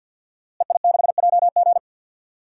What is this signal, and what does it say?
Morse code, heard as a keyed tone, about 46 wpm, 700 Hz. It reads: EI6JG
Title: DXCC-100 Call Signs - 5 Characters - Repeated 3x - 50wpm